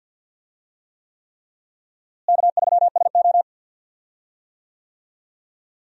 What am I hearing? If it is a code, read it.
Morse code, heard as a keyed tone, about 46 wpm, 700 Hz. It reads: K3SO
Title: USA Call Signs - 4 Characters - Repeated 3x - 50wpm